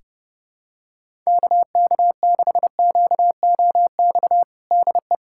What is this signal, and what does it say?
Morse code, heard as a keyed tone, about 28 wpm, 700 Hz. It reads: KK6QOX DE
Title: USA Callsign DE Callsign - 6 Characters - 30wpm